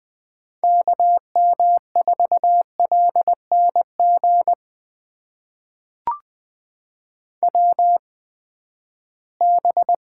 Morse code, heard as a keyed tone, about 20 wpm, spaced as for 14 wpm, 700 Hz. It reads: KM4LNG WB
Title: USA Call Signs - 6 Characters - 8x Character Spacing - 20wpm